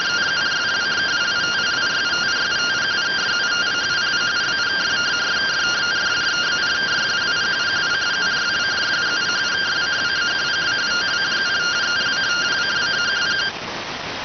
Two-tone teleprinter signal, about 170 Hz shift.